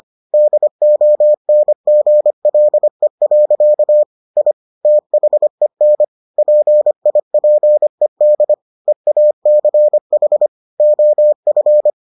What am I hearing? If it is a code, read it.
Morse code, heard as a keyed tone, about 25 wpm, 600 Hz. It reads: DONGLE. I THEN PIPED EACH OF